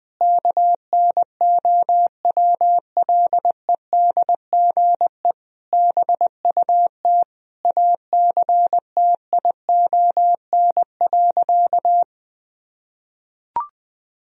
Morse code, heard as a keyed tone, about 21 wpm, 700 Hz. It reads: KNOWLEDGE BUT ACTION.
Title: Sentences from Top 1000 Words - Review Part 4 - 20wpm